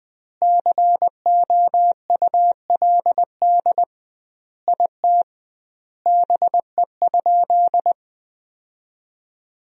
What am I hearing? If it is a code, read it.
Morse code, heard as a keyed tone, about 21 wpm, 700 Hz. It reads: COULD IT BE?
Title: 2x Word-Spacing - Sentences from Top 100 Words - 20wpm